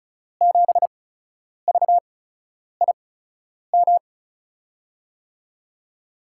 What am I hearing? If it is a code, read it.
Morse code, heard as a keyed tone, about 36 wpm, spaced as for 10 wpm, 700 Hz. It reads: 7VIM
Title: USA Call Signs - 6 Characters - 8x Character Spacing - 35wpm